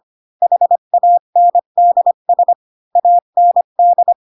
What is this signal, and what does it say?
Morse code, heard as a keyed tone, about 25 wpm, 700 Hz. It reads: HANDS AND